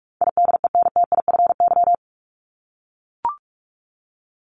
Morse code, heard as a keyed tone, about 44 wpm, 700 Hz. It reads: IDENTIFY
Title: Top 751-1000 Words 45wpm